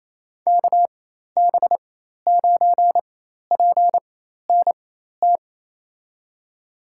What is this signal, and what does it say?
Morse code, heard as a keyed tone, about 28 wpm, spaced as for 13 wpm, 700 Hz. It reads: KB9PNT
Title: USA Call Signs - 6 Characters - 4x Character Spacing - 28wpm